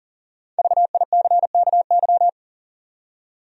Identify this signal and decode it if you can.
Morse code, heard as a keyed tone, about 40 wpm, 700 Hz. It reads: VICKY
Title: USA Common Names - Max 5 Letters - 40wpm